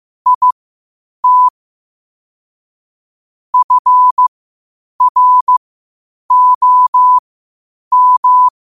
Morse code, heard as a keyed tone, about 15 wpm, spaced as for 8 wpm, 1000 Hz. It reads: IT FROM
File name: qotdfiles / 8wpm